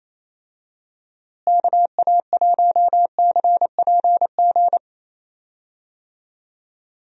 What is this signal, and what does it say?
Morse code, heard as a keyed tone, about 28 wpm, 700 Hz. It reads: KA1CPG
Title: USA Call Signs - 6 Characters - Repeated 2x - 28wpm